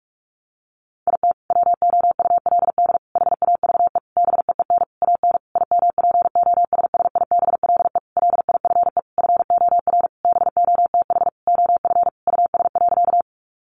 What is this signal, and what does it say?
Morse code, heard as a keyed tone, about 44 wpm, 700 Hz. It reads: IT WOULD HAVE BEEN AN IMPOSSIBLE LIFE FOR BOTH OF US.
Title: Sentences from Top 1100 Words 45wpm